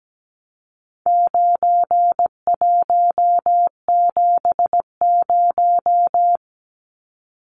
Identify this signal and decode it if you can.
Morse code, heard as a keyed tone, about 17 wpm, 700 Hz. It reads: 9170
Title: Numbers - 4-Digit 17wpm